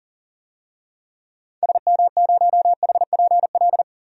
Morse code, heard as a keyed tone, about 39 wpm, 700 Hz. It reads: SM0HPL
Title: DXCC-100 Call Signs - 6 Characters - 40wpm